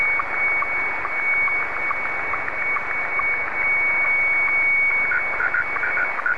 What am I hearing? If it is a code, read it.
SSTV (slow-scan television), Scottie 1.